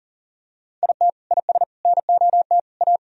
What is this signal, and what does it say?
Morse code, heard as a keyed tone, about 37 wpm, 700 Hz. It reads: IT IS NOT A
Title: Sentences from Top 1000 Words - Review Part 1 - 40wpm